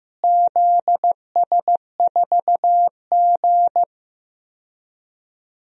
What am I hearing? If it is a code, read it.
Morse code, heard as a keyed tone, about 15 wpm, 700 Hz. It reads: ZS4G
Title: DXCC-100 Call Signs - 4 Characters - Repeated 2x - 15wpm